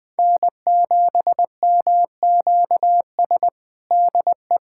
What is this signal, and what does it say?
Morse code, heard as a keyed tone, about 20 wpm, 700 Hz. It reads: N7MQS DE
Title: USA Callsign DE Callsign 20wpm